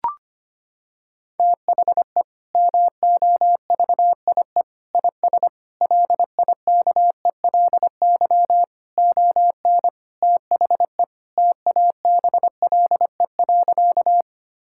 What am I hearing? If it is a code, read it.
Morse code, heard as a keyed tone, about 25 wpm, 700 Hz. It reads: THE MOVIE IS LIKELY ON THE TABLE.
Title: Sentences from Top 1000 Words - Review Part 5 - 25wpm